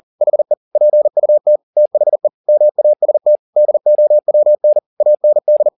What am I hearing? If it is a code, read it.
Morse code, heard as a keyed tone, about 40 wpm, 600 Hz. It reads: HE PUT THE MAST DOWN AND
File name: the old man and the sea 2 40wpm